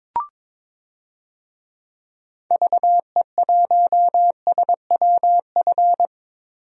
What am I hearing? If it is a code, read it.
Morse code, heard as a keyed tone, about 22 wpm, 700 Hz. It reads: VE1SWF
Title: DXCC-100 Call Signs - 6 Characters - Repeated 2x - 22wpm